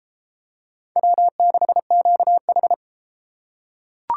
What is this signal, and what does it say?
Morse code, heard as a keyed tone, about 31 wpm, 700 Hz. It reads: W6QH